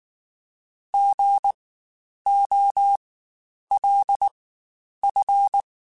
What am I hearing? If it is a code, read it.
Morse code, heard as a keyed tone, about 19 wpm, spaced as for 9 wpm, 800 Hz. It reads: GOLF
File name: Spaced Alphas / Spaced Alphas mp3s / Gs sp @18